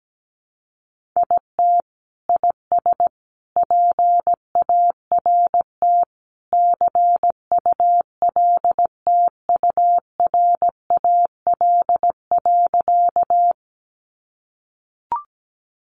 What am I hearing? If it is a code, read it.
Morse code, heard as a keyed tone, about 17 wpm, 700 Hz. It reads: IT IS PART CULTURAL.
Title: Sentences from Top 1500 Words 17wpm